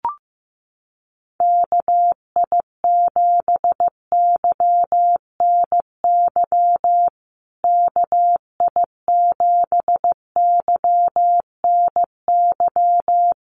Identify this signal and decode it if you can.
Morse code, heard as a keyed tone, about 15 wpm, 700 Hz. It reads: KI7YNY KI7YNY
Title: USA Call Signs Repeated 3x 15wpm